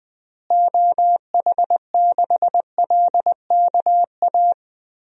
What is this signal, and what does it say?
Morse code, heard as a keyed tone, about 20 wpm, 700 Hz. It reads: OH6LKA
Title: DXCC-100 Call Signs - 6 Characters - Repeated 3x - 20wpm